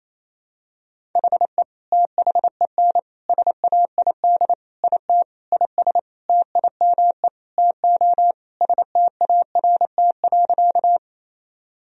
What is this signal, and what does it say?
Morse code, heard as a keyed tone, about 28 wpm, 700 Hz. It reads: HE THEN SAID IT IS TIME TO START.
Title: Sentences from Top 1000 Words - Review Part 1 - 28wpm